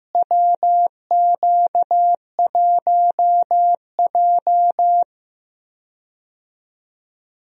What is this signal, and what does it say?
Morse code, heard as a keyed tone, about 15 wpm, 700 Hz. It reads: WQ1J
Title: USA Call Signs - 4 Characters - Repeated 3x - 15wpm